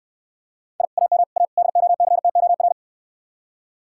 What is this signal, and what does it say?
Morse code, heard as a keyed tone, about 68 wpm, 700 Hz. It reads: I AM A DOCTOR